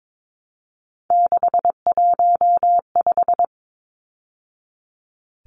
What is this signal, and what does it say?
Morse code, heard as a keyed tone, about 22 wpm, 700 Hz. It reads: 615
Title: Numbers - 3-Digit 22wpm